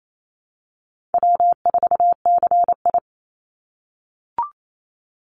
Morse code, heard as a keyed tone, about 28 wpm, 700 Hz. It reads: W4CI
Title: Easy Call Signs - 28wpm